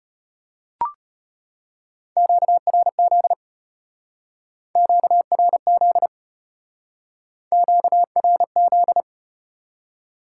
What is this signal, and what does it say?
Morse code, heard as a keyed tone, about 33 wpm, 700 Hz. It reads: QRZ QRZ QRZ